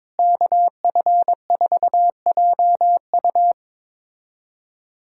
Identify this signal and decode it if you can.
Morse code, heard as a keyed tone, about 22 wpm, 700 Hz. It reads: KF4JU
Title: USA Callsign DE Callsign 22wpm